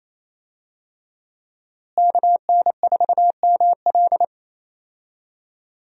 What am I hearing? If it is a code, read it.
Morse code, heard as a keyed tone, about 28 wpm, 700 Hz. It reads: KN4ML
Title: USA Call Signs - 5 Characters - Repeated 2x - 28wpm